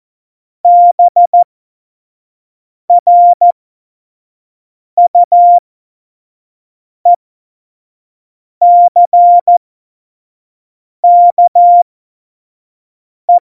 Morse code, heard as a keyed tone, about 14 wpm, spaced as for 5 wpm, 700 Hz. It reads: BRUECKE
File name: Morsekurs / Witze / Witz 21. Mai 2019, 25 F70